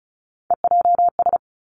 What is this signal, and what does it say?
Morse code, heard as a keyed tone, about 36 wpm, 700 Hz. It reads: EJS
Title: Three Challenging Letter-Number Combinations 35wpm